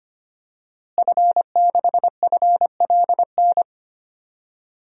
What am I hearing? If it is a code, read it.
Morse code, heard as a keyed tone, about 25 wpm, 700 Hz. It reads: F6FLN
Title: DXCC-100 Call Signs - 5 Characters - Repeated 3x - 25wpm